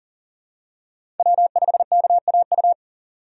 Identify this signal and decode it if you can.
Morse code, heard as a keyed tone, about 40 wpm, 700 Hz. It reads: W5KAU
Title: USA Call Signs - 5 Characters - 40wpm